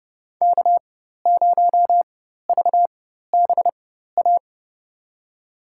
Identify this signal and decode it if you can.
Morse code, heard as a keyed tone, about 32 wpm, spaced as for 14 wpm, 700 Hz. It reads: K0VBA
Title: USA Call Signs - 4x Character Spacing - 30wpm